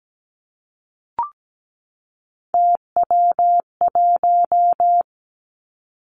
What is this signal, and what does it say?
Morse code, heard as a keyed tone, about 17 wpm, 700 Hz. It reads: TW1
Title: Three Challenging Letter-Number Combinations 17wpm